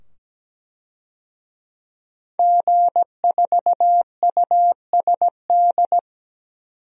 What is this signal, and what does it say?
Morse code, heard as a keyed tone, about 17 wpm, 700 Hz. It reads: G4USD
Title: DXCC-100 Call Signs - 5 Characters - 17wpm